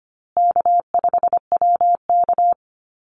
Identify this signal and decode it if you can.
Morse code, heard as a keyed tone, about 25 wpm, 700 Hz. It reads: K5WK
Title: Easy Call Signs Repeated 3x - 25wpm